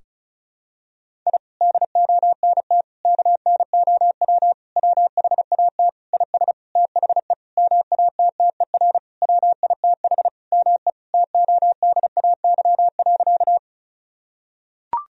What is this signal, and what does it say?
Morse code, heard as a keyed tone, about 36 wpm, 700 Hz. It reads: I DONT KNOW WHAT IS THE MATTER WITH ME TODAY.
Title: Sentences from Top 1000 Words - Review Part 4 - 35wpm